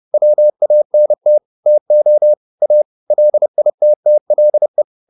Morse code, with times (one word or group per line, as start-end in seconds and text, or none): 0.1-1.4 WANT
1.7-2.3 TO
2.6-2.8 A
3.1-4.8 LITTLE